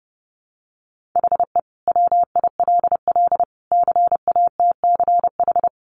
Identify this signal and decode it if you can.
Morse code, heard as a keyed tone, about 30 wpm, 700 Hz. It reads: HE WILL CATCH
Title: Sentences from Top 1500 Words 30wpm